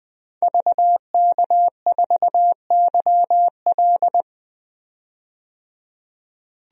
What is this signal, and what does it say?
Morse code, heard as a keyed tone, about 20 wpm, 700 Hz. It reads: VK4YL
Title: DXCC-100 Call Signs - 5 Characters - 20wpm